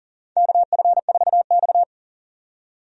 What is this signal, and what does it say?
Morse code, heard as a keyed tone, about 40 wpm, 700 Hz. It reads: KF4X
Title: USA Call Signs - 4 Characters - 40wpm